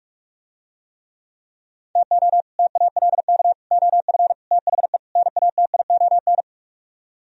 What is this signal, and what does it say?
Morse code, heard as a keyed tone, about 49 wpm, 700 Hz. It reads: TO TALK OF THE NATION